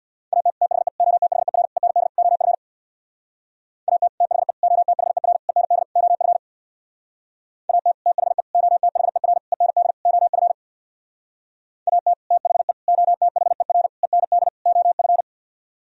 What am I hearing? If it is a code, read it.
Morse code, heard as a keyed tone, about 64 wpm, 700 Hz. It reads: AT THE OTHER END OF AT THE OTHER END OF AT THE OTHER END OF AT THE OTHER END OF